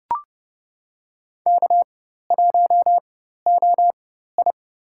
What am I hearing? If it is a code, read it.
Morse code, heard as a keyed tone, about 30 wpm, spaced as for 14 wpm, 700 Hz. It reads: K1OI
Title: USA Call Signs - 4x Character Spacing - 30wpm